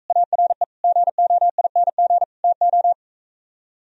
Morse code, heard as a keyed tone, about 45 wpm, 700 Hz. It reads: ARE GOING TO